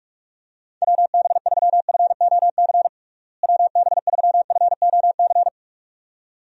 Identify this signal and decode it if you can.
Morse code, heard as a keyed tone, about 46 wpm, 700 Hz. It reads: WB3FOC WB3FOC
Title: USA Call Signs - 6 Characters - Repeated 2x - 45wpm